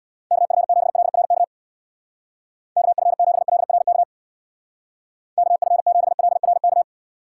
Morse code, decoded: KF7LRK KF7LRK KF7LRK

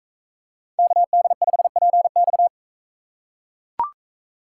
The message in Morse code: KD5PX

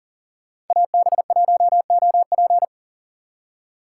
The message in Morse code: AB1OP

